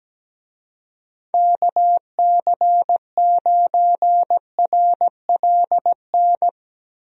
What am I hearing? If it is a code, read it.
Morse code, heard as a keyed tone, about 17 wpm, 700 Hz. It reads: KC9RLN
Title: USA Call Signs - 6 Characters - Repeated 2x - 17wpm